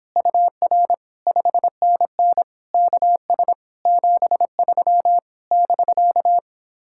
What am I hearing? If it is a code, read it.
Morse code, heard as a keyed tone, about 29 wpm, 700 Hz. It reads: UR 5NN KS 73 <BK>